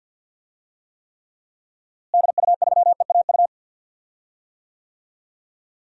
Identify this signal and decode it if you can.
Morse code, heard as a keyed tone, about 52 wpm, 700 Hz. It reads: DU3EAU
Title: DXCC-100 Call Signs - 6 Characters - Repeated 2x - 50wpm